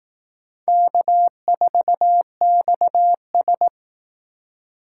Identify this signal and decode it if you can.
Morse code, heard as a keyed tone, about 19 wpm, 700 Hz. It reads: K4XS